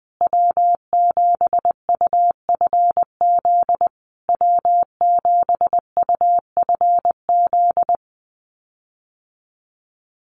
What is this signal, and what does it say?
Morse code, heard as a keyed tone, about 20 wpm, 700 Hz. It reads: W7UFZ W7UFZ
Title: USA Call Signs Repeated 3x 20wpm